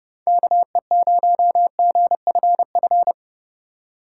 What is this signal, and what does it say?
Morse code, heard as a keyed tone, about 30 wpm, 700 Hz. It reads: KE0GFF
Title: USA Call Signs - 6 Characters - 2x Character Spacing - 30wpm